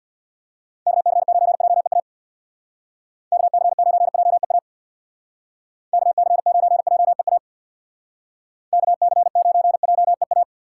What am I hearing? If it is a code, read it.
Morse code, heard as a keyed tone, about 62 wpm, 700 Hz. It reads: KC9JEA KC9JEA KC9JEA KC9JEA
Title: Speed Racing - USA 2x3 Call Signs 50wpm